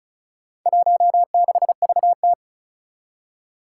Morse code, heard as a keyed tone, about 36 wpm, 700 Hz. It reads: 16VT
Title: Four Challenging Letter-Number Combinations 35wpm